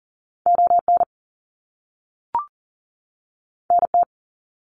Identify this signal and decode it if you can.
Morse code, heard as a keyed tone, about 40 wpm, 700 Hz. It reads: ON NT
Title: Canadian Provinces and Territories 40wpm